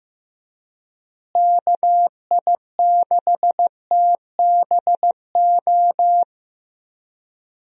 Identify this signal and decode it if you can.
Morse code, heard as a keyed tone, about 15 wpm, 700 Hz. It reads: KI6TBO
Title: USA Call Signs - 6 Characters - Repeated 3x - 15wpm